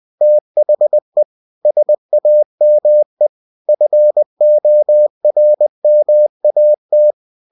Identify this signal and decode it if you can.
Morse code, heard as a keyed tone, about 20 wpm, 600 Hz. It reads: THE SAME FORMAT
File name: cw-books / winnie-the-pooh / winnie-20wpm-page-0114